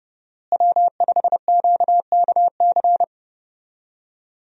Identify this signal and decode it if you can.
Morse code, heard as a keyed tone, about 30 wpm, 700 Hz. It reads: W5QKC
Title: USA Call Signs - 5 Characters - Repeated 3x - 30wpm